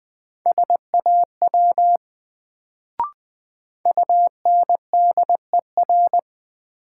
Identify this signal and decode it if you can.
Morse code, heard as a keyed tone, about 21 wpm, 700 Hz. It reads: SAW UNDER
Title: Sentences from Top 300 Words - New Word Drill - 20wpm